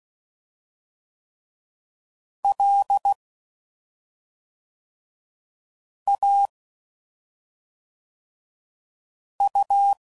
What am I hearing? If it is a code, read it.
Morse code, heard as a keyed tone, about 16 wpm, 800 Hz. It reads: L A U
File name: Suffixes / Special ING's / SlowToFaster ING's